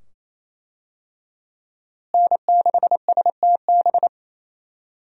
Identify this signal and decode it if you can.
Morse code, heard as a keyed tone, about 28 wpm, 700 Hz. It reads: N6STB
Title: USA Call Signs - 5 Characters - Repeated 2x - 28wpm